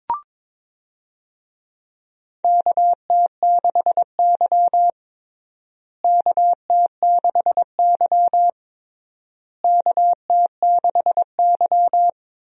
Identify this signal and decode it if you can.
Morse code, heard as a keyed tone, about 22 wpm, 700 Hz. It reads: KT6Y KT6Y KT6Y
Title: USA Call Signs - 4 Characters - Repeated 3x - 22wpm